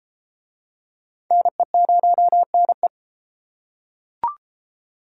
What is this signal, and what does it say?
Morse code, heard as a keyed tone, about 33 wpm, 700 Hz. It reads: NE0NE